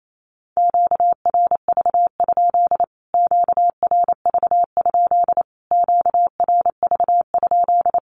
Morse code, heard as a keyed tone, about 28 wpm, 700 Hz. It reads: QRV? QRV? QRV?
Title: Q-Codes Repeated 3x 28wpm